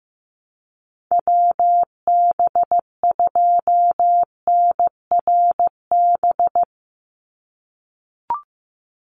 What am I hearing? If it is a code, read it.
Morse code, heard as a keyed tone, about 15 wpm, 700 Hz. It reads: WB2NRB
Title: USA Call Signs 15wpm